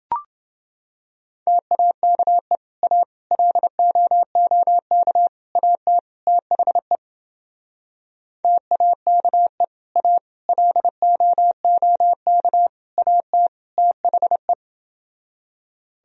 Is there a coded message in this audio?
Yes: Morse code from 1.5 to 14.5 s, heard as a keyed tone, about 28 wpm, 700 Hz. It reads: TAKE A LOOK AT THE TAKE A LOOK AT THE